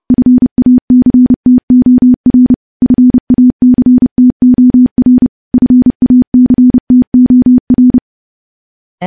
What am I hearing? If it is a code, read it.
Morse code, heard as a keyed tone, about 30 wpm, 250 Hz. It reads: FACTOR FACTOR FACTOR
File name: morse / morse-training-30WPM